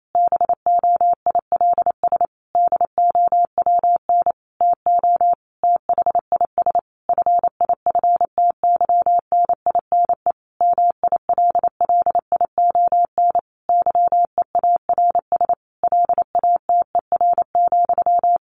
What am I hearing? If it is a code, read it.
Morse code, heard as a keyed tone, about 28 wpm, 700 Hz. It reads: BOILS DOWN TO THIS FIFTYNINE MILLION YEARS LATER,
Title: A Sound Of Thunder 28wpm